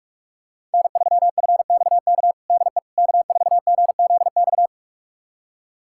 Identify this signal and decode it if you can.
Morse code, heard as a keyed tone, about 41 wpm, 700 Hz. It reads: N3FXK DE K4GZX
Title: USA Callsign DE Callsign - 5 Characters - 45wpm